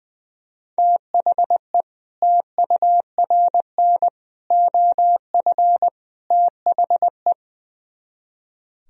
Morse code, heard as a keyed tone, about 23 wpm, 700 Hz. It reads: THE TURN OF THE
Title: Speed-Racing - Sets of 4 Words 20wpm